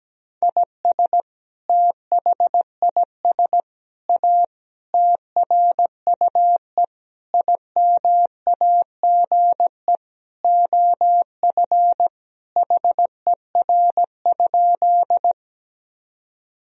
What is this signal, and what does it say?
Morse code, heard as a keyed tone, about 17 wpm, 700 Hz. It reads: IS THIS A TRUE IMAGE OF HER?
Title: Sentences from Top 1000 Words - Review Part 4 - 17wpm